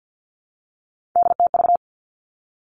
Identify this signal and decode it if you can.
Morse code, heard as a keyed tone, about 50 wpm, 700 Hz. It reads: DTV
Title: Three Challenging Letter-Number Combinations 50wpm